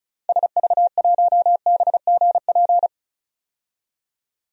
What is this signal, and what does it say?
Morse code, heard as a keyed tone, about 34 wpm, 700 Hz. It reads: SV1BGP
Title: DXCC-100 Call Signs - 6 Characters - 35wpm